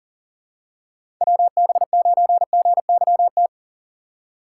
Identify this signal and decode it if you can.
Morse code, heard as a keyed tone, about 38 wpm, 700 Hz. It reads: WB9GYT
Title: USA Call Signs - 6 Characters - 3x Character Spacing - 40wpm